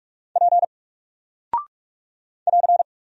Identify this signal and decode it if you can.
Morse code, heard as a keyed tone, about 44 wpm, 700 Hz. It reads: P <AR>
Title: Single Character (Letter:Number:Punctuation:Prosign) Alt 45wpm